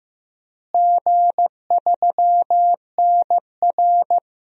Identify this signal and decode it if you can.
Morse code, heard as a keyed tone, about 15 wpm, 700 Hz. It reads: G3NR
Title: DXCC-100 Call Signs - 4 Characters - Repeated 3x - 15wpm